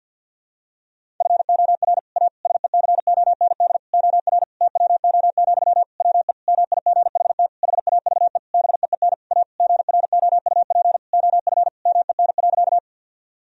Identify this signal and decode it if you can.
Morse code, heard as a keyed tone, about 51 wpm, 700 Hz. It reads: FOR A SECOND OR TWO, WE MIGHT HAVE BEEN A GROUP OF MEN.
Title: Sentences from Top 1000 Words - Review Part 1 - 50wpm